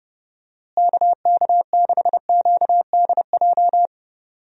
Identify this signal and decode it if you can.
Morse code, heard as a keyed tone, about 30 wpm, 700 Hz. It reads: KK6QDJ
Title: USA Call Signs - 6 Characters - Repeated 3x - 30wpm